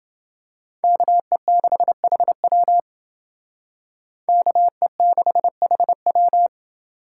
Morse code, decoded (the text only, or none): KE6HW KE6HW